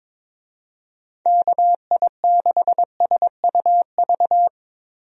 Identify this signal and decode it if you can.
Morse code, heard as a keyed tone, about 22 wpm, 700 Hz. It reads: KI6SUV
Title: USA Call Signs - 6 Characters - Repeated 2x - 22wpm